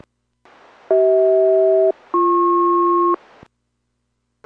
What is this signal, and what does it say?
Aircraft SELCAL, heard as two pairs of tones. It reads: CH-BP